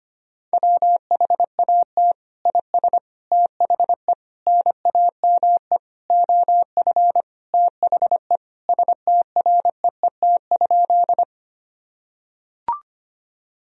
Morse code, heard as a keyed tone, about 25 wpm, 700 Hz. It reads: WHAT IS THE NAME OF THE STREET?
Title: Sentences from Top 1000 Words - Review Part 3 - 25wpm